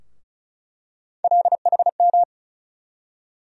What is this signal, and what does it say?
Morse code, heard as a keyed tone, about 34 wpm, 700 Hz. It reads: LHM
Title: USA Call Sign Postfixes - 3 Letters 35wpm